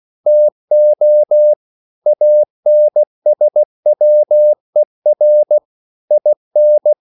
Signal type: Morse code